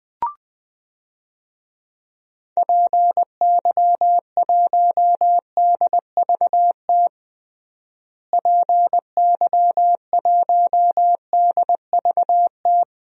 Morse code, heard as a keyed tone, about 19 wpm, 700 Hz. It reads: PY1DVT PY1DVT
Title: DXCC-100 Call Signs - 6 Characters - Repeated 3x - 20wpm